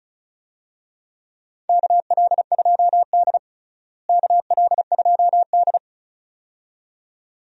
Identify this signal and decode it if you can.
Morse code, heard as a keyed tone, about 35 wpm, 700 Hz. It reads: KL2D KL2D
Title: USA Call Signs - 4 Characters - Repeated 2x - 35wpm